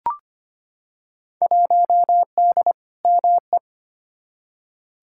Morse code, heard as a keyed tone, about 25 wpm, 700 Hz. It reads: 1D ME
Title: Field Day Exchanges - 25wpm